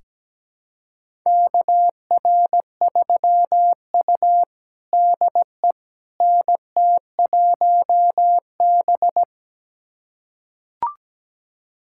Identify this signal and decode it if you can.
Morse code, heard as a keyed tone, about 17 wpm, 700 Hz. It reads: KR3U DE NT1B
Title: USA Callsign DE Callsign - 4 Characters - 17wpm